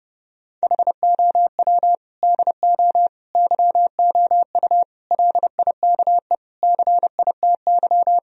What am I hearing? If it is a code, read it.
Morse code, heard as a keyed tone, about 30 wpm, 700 Hz. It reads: HOW DO YOU LIKE CITY
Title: Sentences from Top 1000 Words - Review Part 1 - 30wpm